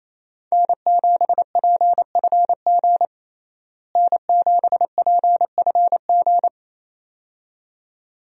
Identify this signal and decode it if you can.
Morse code, heard as a keyed tone, about 28 wpm, 700 Hz. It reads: N7PFG N7PFG
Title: USA Call Signs - 5 Characters - Repeated 2x - 28wpm